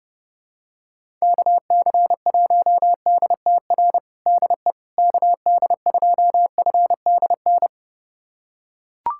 Morse code, heard as a keyed tone, about 30 wpm, 700 Hz. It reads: KC1DTR DE KD2FDN
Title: USA Callsign DE Callsign 30wpm